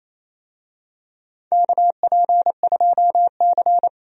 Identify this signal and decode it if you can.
Morse code, heard as a keyed tone, about 28 wpm, 700 Hz. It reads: KP2C
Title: DXCC-100 Call Signs - 4 Characters - 28wpm